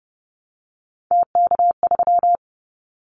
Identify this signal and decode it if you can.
Morse code, heard as a keyed tone, about 30 wpm, 700 Hz. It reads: TK3
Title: Callsign Prefixes for Top 100 DXCC - 30wpm